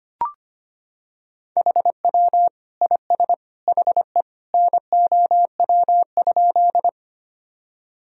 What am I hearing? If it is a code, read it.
Morse code, heard as a keyed tone, about 25 wpm, 700 Hz. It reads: HW IS HE NOW?
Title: Sentences from Top 100 Words and Abbreviations 25wpm